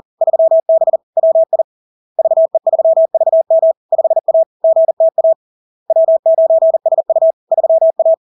Morse code, heard as a keyed tone, about 40 wpm, 650 Hz. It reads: 3B WI VE3VM 5A GTA W9SU 3A
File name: LARK-FD-2022-Log-in-CW-40wpm-650-Hz